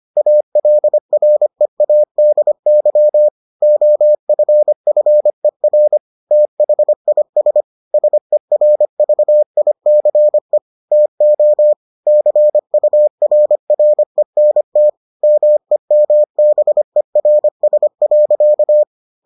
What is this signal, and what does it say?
Morse code, heard as a keyed tone, about 25 wpm, 600 Hz. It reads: ALREADY OFFER THIS SERVICE TO CURRENT MEMBERS.